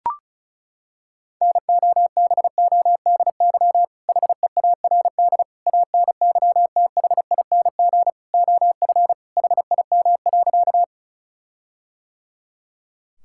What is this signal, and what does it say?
Morse code, heard as a keyed tone, about 36 wpm, 700 Hz. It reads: NOBODY HEARD ANYTHING OF HIM.
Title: Sentences from Top 1600 Words 35wpm